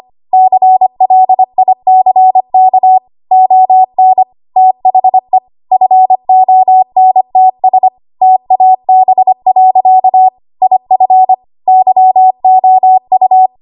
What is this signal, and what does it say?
Morse code, heard as a keyed tone, about 25 wpm, 750 Hz. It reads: CLICK ON THE FONTS TAB. IF YOU
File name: Archive / 25 WPM / 25WPM